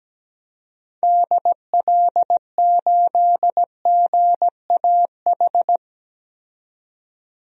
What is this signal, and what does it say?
Morse code, heard as a keyed tone, about 17 wpm, 700 Hz. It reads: DL8GAH
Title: DXCC-100 Call Signs - 6 Characters - 17wpm